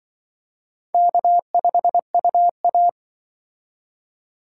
Morse code, heard as a keyed tone, about 25 wpm, 700 Hz. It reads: K5UA